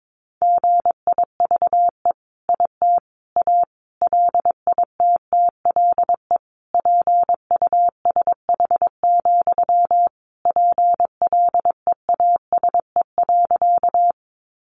Morse code, heard as a keyed tone, about 22 wpm, 700 Hz. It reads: GIVE IT A LITTLE PUSH, PLEASE.
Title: Sentences from Top 1500 Words 22wpm